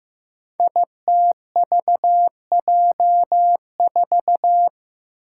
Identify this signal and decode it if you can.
Morse code, heard as a keyed tone, about 15 wpm, 700 Hz. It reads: ITVJ4
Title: Five Challenging Letter-Number Combinations 15wpm